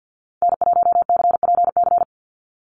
Morse code, heard as a keyed tone, about 50 wpm, 700 Hz. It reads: N1CPC
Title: USA Call Signs 50wpm